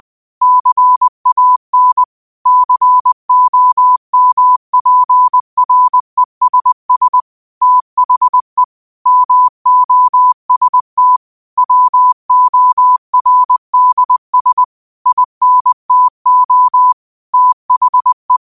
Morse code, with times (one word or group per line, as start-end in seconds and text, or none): 0.4-2.0 CAN
2.5-7.2 COMPRESS
7.6-8.6 THE
9.1-11.2 MOST
11.6-14.6 WORDS
15.1-16.9 INTO
17.3-18.4 THE